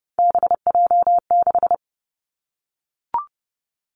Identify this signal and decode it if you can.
Morse code, heard as a keyed tone, about 30 wpm, 700 Hz. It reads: BJ6
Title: Three Challenging Letter-Number Combinations 30wpm